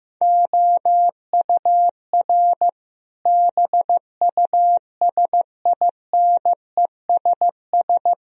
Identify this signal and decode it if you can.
Morse code, heard as a keyed tone, about 15 wpm, 700 Hz. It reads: OUR BUSINESS
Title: Sentences from Top 1000 Words - Review Part 4 - 15wpm